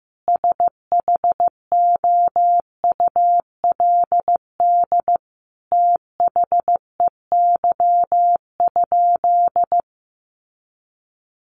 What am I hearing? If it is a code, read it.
Morse code, heard as a keyed tone, about 15 wpm, 700 Hz. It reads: SHOULD THEY?
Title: Sentences from Top 300 Words 15wpm